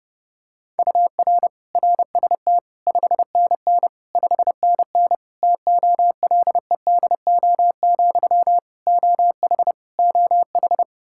Morse code, heard as a keyed tone, about 32 wpm, 700 Hz. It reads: UR RST 5NN 5NN TOLEDO, OH OH
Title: Practice QSO - Exchange 1 - Beginning - 30wpm